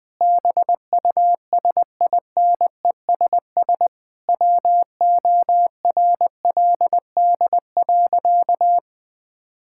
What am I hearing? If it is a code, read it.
Morse code, heard as a keyed tone, about 21 wpm, 700 Hz. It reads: BUSINESS WORLD.
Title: Sentences from Top 1000 Words - Review Part 5 - 20wpm